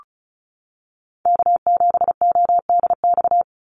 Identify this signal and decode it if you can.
Morse code, heard as a keyed tone, about 34 wpm, 700 Hz. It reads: K7ODX
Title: NA SOTA Chaser Callsigns 35wpm